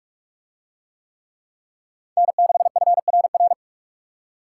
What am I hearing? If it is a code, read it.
Morse code, heard as a keyed tone, about 41 wpm, 700 Hz. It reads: N6FRR